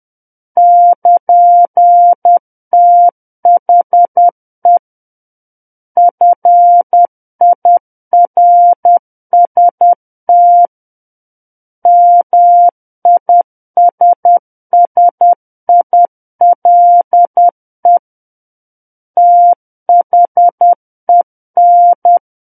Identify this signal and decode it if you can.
Morse code, heard as a keyed tone, about 10 wpm, 700 Hz. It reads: <KN>THE FIRST MISSILE THEN
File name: Book One / 01-Chapter 1 - 10 WPM